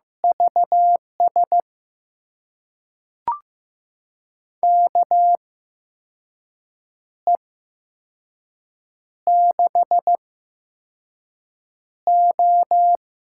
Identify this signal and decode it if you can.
Morse code, heard as a keyed tone, about 15 wpm, spaced as for 5 wpm, 700 Hz. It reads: VS KE6O
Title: USA Call Signs - 6 Characters - 8x Character Spacing - 15wpm